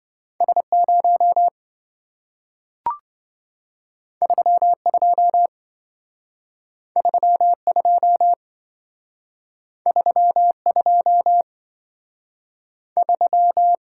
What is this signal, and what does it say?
Morse code, heard as a keyed tone, about 27 wpm, 700 Hz. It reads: S0 32 32 32 3